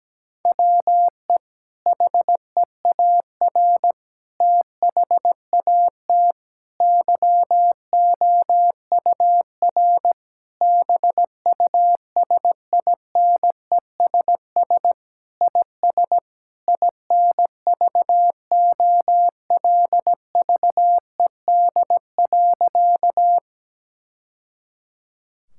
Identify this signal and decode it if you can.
Morse code, heard as a keyed tone, about 17 wpm, 700 Hz. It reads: WE HEAR THAT YOUR BUSINESS IS INVOLVED.
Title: Sentences from Top 1000 Words - Review Part 5 - 17wpm